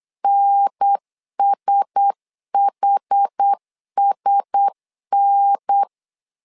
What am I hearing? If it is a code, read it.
Morse code, heard as a keyed tone, about 8 wpm, 800 Hz. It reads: NSHSN
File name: LEZIONI / 4 Q1S 8wpm 2710